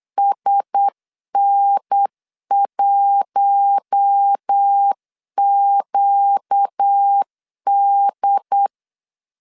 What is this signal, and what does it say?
Morse code, heard as a keyed tone, about 8 wpm, 800 Hz. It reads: SN1QD
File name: LEZIONI / 7 GF9Z / 7 GF 8wpm 3108